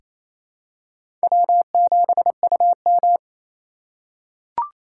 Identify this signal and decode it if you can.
Morse code, heard as a keyed tone, about 28 wpm, 700 Hz. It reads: W7UM
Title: USA Call Signs - 4 Characters - 4x Character Spacing - 28wpm